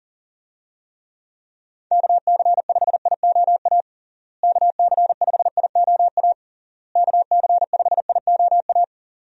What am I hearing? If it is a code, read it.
Morse code, heard as a keyed tone, about 40 wpm, 700 Hz. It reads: KC5IOA KC5IOA KC5IOA
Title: USA Call Signs - 6 Characters - Repeated 3x - 40wpm